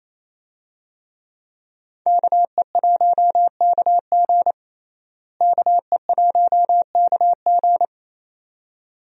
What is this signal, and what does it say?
Morse code, heard as a keyed tone, about 28 wpm, 700 Hz. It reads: KE1KG KE1KG
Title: USA Call Signs - 5 Characters - Repeated 2x - 28wpm